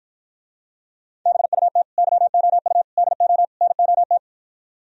Morse code, heard as a keyed tone, about 53 wpm, 700 Hz. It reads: BUT YOU DO NOT